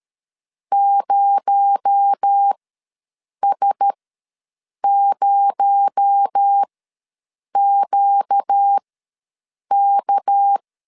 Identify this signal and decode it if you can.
Morse code, heard as a keyed tone, about 13 wpm, spaced as for 7 wpm, 800 Hz. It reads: 0S0QK
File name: LEZIONI / 6 PKX0 / 6 PKX0 ws 12wpm 3448